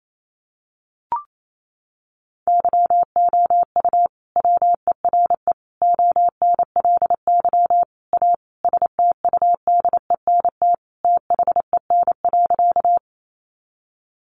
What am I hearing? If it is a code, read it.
Morse code, heard as a keyed tone, about 28 wpm, 700 Hz. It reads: YOU WERE ONLY A STUDENT THEN.
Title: Sentences from Top 1000 Words 28wpm